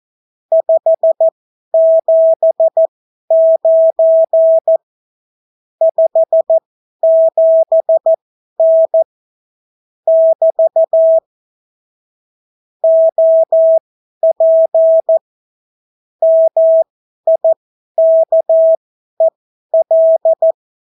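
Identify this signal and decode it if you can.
Morse code, heard as a keyed tone, about 14 wpm, spaced as for 11 wpm, 650 Hz. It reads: 579 57N <BT> OP MIKEL
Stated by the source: Example Spanish CW QSO from a Morse learning book, 14 wpm characters with 8 wpm spacing.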